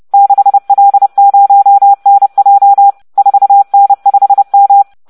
Morse code, heard as a keyed tone, about 30 wpm, 800 Hz. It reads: 6L0NJ 4N5M